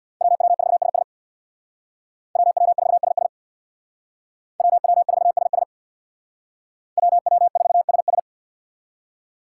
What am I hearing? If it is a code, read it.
Morse code, heard as a keyed tone, about 58 wpm, 700 Hz. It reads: WW4SS WW4SS WW4SS WW4SS
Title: Speed Racing - USA 2x2 Call Signs 50wpm